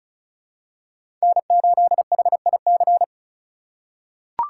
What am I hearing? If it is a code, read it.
Morse code, heard as a keyed tone, about 36 wpm, 700 Hz. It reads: N8HIC
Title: USA Call Signs - 5 Characters - 4x Character Spacing - 35wpm